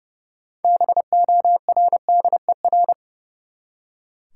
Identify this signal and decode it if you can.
Morse code, heard as a keyed tone, about 30 wpm, 700 Hz. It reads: BORDER
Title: Six Letter Words - 30wpm